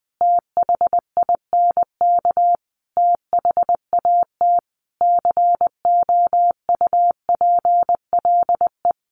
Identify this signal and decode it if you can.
Morse code, heard as a keyed tone, about 20 wpm, 700 Hz. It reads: THINK THAT COUPLE
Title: Sentences from Top 1100 Words 20wpm